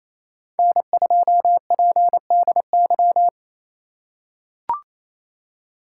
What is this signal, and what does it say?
Morse code, heard as a keyed tone, about 28 wpm, 700 Hz. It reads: N2PDY
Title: USA Call Signs - 3x Character Spacing - 28wpm